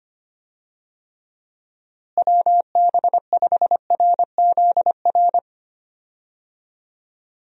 Morse code, heard as a keyed tone, about 25 wpm, 700 Hz. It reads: WB5RZR
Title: USA Call Signs - 6 Characters - Repeated 3x - 25wpm